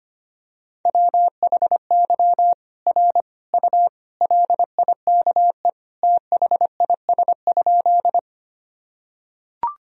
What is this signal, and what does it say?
Morse code, heard as a keyed tone, about 25 wpm, 700 Hz. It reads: WHY R U LIKE THIS?
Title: Sentences from Top 200 Words and Abbreviations 25wpm